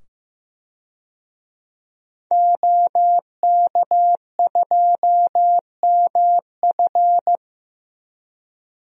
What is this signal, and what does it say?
Morse code, heard as a keyed tone, about 15 wpm, 700 Hz. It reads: OK2MF
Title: DXCC-100 Call Signs - 5 Characters - 15wpm